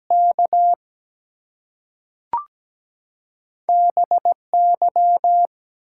Morse code, heard as a keyed tone, about 17 wpm, 700 Hz. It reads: K BY
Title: Two Letter Words - 17wpm